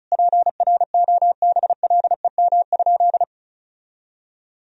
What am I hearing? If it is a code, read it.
Morse code, heard as a keyed tone, about 36 wpm, 700 Hz. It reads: PROBLEM?
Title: Sentences from Top 1000 Words - Review Part 4 - 35wpm